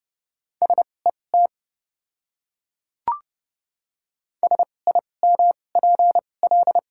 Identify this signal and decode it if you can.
Morse code, heard as a keyed tone, about 30 wpm, spaced as for 22 wpm, 690 Hz. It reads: SET SIMPL
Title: 2x Character Spacing - Top 500 Words - 30wpm